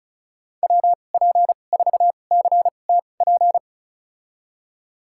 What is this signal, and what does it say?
Morse code, heard as a keyed tone, about 36 wpm, spaced as for 25 wpm, 700 Hz. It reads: WP4CTP
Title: USA Call Signs - 6 Characters - 2x Character Spacing - 35wpm